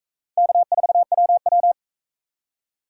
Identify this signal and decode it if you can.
Morse code, heard as a keyed tone, about 41 wpm, 700 Hz. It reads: K4WW